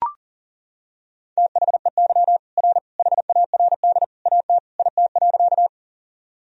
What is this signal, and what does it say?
Morse code, heard as a keyed tone, about 40 wpm, 700 Hz. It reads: THEY R HARD AT IT.
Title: Sentences from Top 300 Words and Abbreviations 40wpm